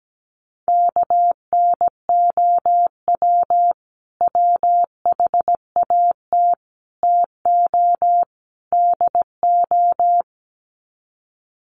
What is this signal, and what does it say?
Morse code, heard as a keyed tone, about 17 wpm, 700 Hz. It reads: KNOW WHAT TO DO
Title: Sets of 4 Words 17wpm